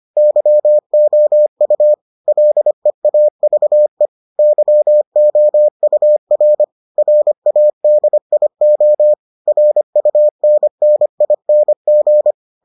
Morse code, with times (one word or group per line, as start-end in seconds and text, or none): 0.2-1.9 YOU
2.3-4.1 LEAVE
4.4-6.6 YOUR
7.0-9.1 RADIO
9.5-12.3 RUNNING